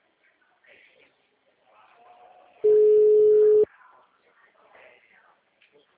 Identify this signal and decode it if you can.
Ringback tone (Europe, 425 Hz), with speech.
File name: out- -2017- - - .64868